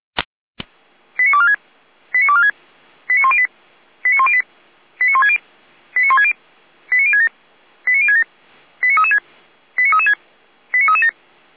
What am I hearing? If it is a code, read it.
Five-tone selective call (ZVEI), heard as quick runs of five tones, tones of about 70 ms. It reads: 89246 89246 89108 89108 89160 89160 89067 89067 89336 89336 89337